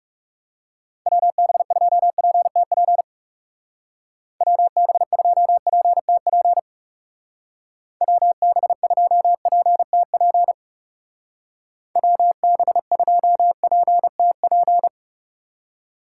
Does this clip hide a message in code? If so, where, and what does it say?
Yes: Morse code from 1.1 to 14.9 s, heard as a keyed tone, about 38 wpm, 700 Hz. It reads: WB2PTP WB2PTP WB2PTP WB2PTP